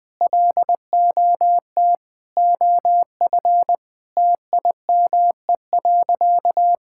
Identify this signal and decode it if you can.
Morse code, heard as a keyed tone, about 20 wpm, 700 Hz. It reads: LOT OF TIME.
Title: Sentences with 5-Word Phrases - Spoken Beginning - 20wpm